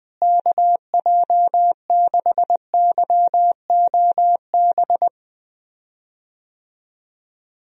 Morse code, heard as a keyed tone, about 20 wpm, 700 Hz. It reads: KJ6YOB
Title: USA Call Signs - 6 Characters - Repeated 2x - 20wpm